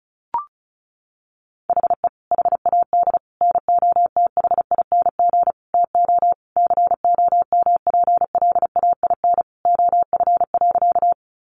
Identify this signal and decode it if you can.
Morse code, heard as a keyed tone, about 34 wpm, 700 Hz. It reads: HE HAD NOTHING TO COMPLAIN OF.
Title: Sentences from Top 1300 Words 35wpm